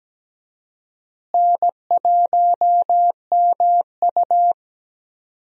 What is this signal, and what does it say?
Morse code, heard as a keyed tone, about 17 wpm, 700 Hz. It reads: N1MU
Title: USA Call Signs - 4 Characters - Repeated 2x - 17wpm